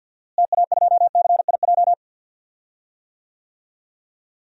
Morse code, heard as a keyed tone, about 50 wpm, 700 Hz. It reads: TA2CIJ
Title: DXCC-100 Call Signs - 6 Characters - Repeated 3x - 50wpm